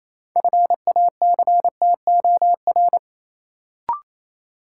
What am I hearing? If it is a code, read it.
Morse code, heard as a keyed tone, about 27 wpm, 700 Hz. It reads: FACTOR
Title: Sentences from Top 1300 Words - New Word Drill Alt - 28wpm